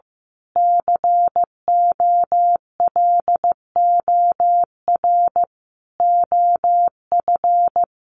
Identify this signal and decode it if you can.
Morse code, heard as a keyed tone, about 15 wpm, 700 Hz. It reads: COLOR OF
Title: A Sound Of Thunder 15wpm